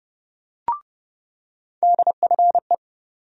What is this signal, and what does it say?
Morse code, heard as a keyed tone, about 30 wpm, 700 Hz. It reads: DFE
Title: USA Call Sign Postfixes - 3 Letters 30wpm